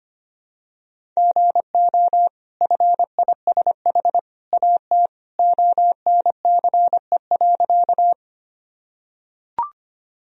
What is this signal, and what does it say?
Morse code, heard as a keyed tone, about 25 wpm, 700 Hz. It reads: GO FISH AT ONCE.
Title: Sentences from Top 1000 Words - Review Part 1 - 25wpm